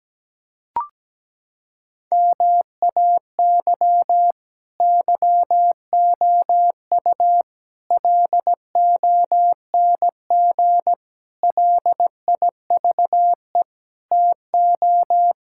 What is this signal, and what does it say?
Morse code, heard as a keyed tone, about 17 wpm, 700 Hz. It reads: MAY YOU LONG LIVE TO
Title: Sentences from Top 1000 Words - Review Part 6 - 17wpm